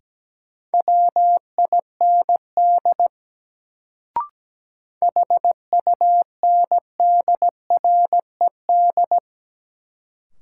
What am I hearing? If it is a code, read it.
Morse code, heard as a keyed tone, about 17 wpm, 700 Hz. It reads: WIND HUNDRED
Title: Sentences from Top 400 Words - New Word Drill - 17wpm